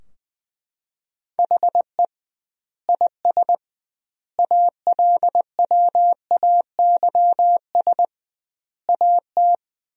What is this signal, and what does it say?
Morse code, heard as a keyed tone, about 20 wpm, 700 Hz. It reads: HE IS ALWAYS AT
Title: 2x Word-Spacing - Sentences from Top 300 Words - 20wpm